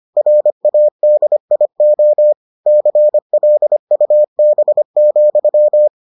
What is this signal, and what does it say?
Morse code, heard as a keyed tone, about 25 wpm, 600 Hz. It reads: RADIO CLUB,